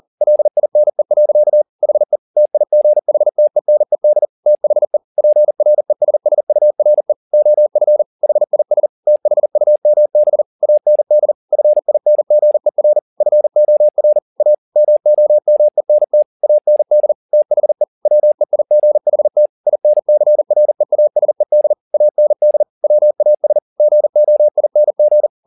Morse code, heard as a keyed tone, about 40 wpm, 600 Hz. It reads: LINE. HE TIGHTENED THE PRESSURE OF HIS THUMB AND FINGER FOR A MOMENT AND THE WEIGHT INCREASED AND WAS GOING